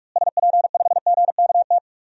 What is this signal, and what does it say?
Morse code, heard as a keyed tone, about 41 wpm, 700 Hz. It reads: SP5GKT
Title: DXCC-100 Call Signs - 6 Characters - Repeated 3x - 45wpm